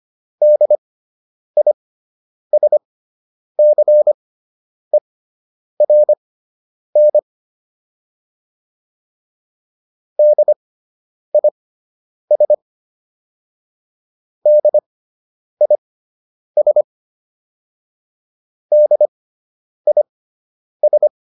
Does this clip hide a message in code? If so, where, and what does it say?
Yes: Morse code from 0.4 to 21.1 s, heard as a keyed tone, about 25 wpm, spaced as for 9 wpm, 600 Hz. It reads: DISCERN DIS DIS DIS